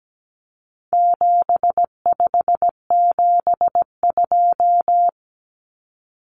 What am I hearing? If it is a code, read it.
Morse code, heard as a keyed tone, about 17 wpm, 700 Hz. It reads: 7572
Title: Numbers - 4-Digit 17wpm